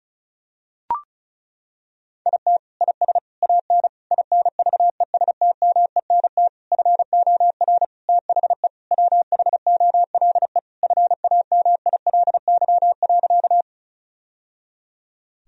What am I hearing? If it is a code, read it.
Morse code, heard as a keyed tone, about 36 wpm, 700 Hz. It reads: IT IS AN INVESTMENT FOR THE WHOLE FAMILY.
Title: Sentences from Top 1000 Words - Review Part 4 - 35wpm